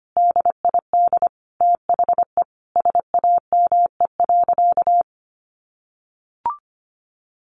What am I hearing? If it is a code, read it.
Morse code, heard as a keyed tone, about 25 wpm, 700 Hz. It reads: DID THE SAME.
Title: Sentences from Top 1500 Words 25wpm